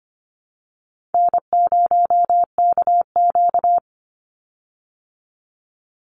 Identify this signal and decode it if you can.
Morse code, heard as a keyed tone, about 25 wpm, 700 Hz. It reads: N0KQ
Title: Easy Call Signs Repeated 3x - 25wpm